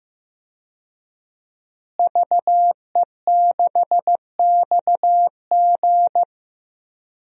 Morse code, heard as a keyed tone, about 15 wpm, 700 Hz. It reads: VE6XG
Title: DXCC-100 Call Signs - 5 Characters - Repeated 2x - 15wpm